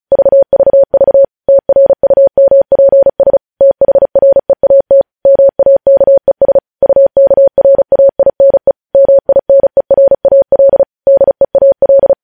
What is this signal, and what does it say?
Morse code, heard as a keyed tone, about 36 wpm, 550 Hz. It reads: VVV TRUMPS THREAT MAKES UKRAINE MINERAL DEAL
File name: podcast / News headlines-1x-35x35